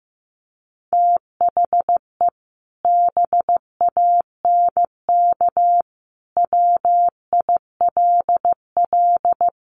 Morse code, heard as a keyed tone, about 15 wpm, 700 Hz. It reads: THE BANK WILL
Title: Sentences from Top 1500 Words 15wpm